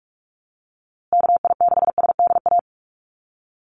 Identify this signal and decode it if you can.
Morse code, heard as a keyed tone, about 44 wpm, 700 Hz. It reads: KI6SDA
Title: USA Call Signs Repeated 3x 45wpm